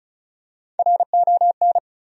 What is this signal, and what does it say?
Morse code, heard as a keyed tone, about 36 wpm, 700 Hz. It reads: RON
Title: USA Common Names - 35wpm